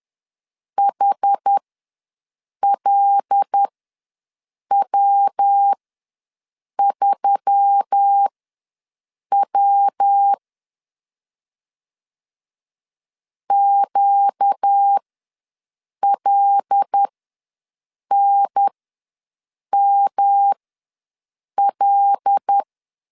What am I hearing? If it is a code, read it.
Morse code, heard as a keyed tone, about 10 wpm, spaced as for 6 wpm, 800 Hz. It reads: HLW3W QLNML
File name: LEZIONI / 5 LBVC / 5 L ws 10wpm 3636